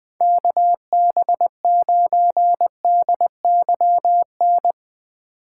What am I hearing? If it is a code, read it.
Morse code, heard as a keyed tone, about 20 wpm, 700 Hz. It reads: KB9DYN
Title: USA Call Signs - 2x Character Spacing - 20wpm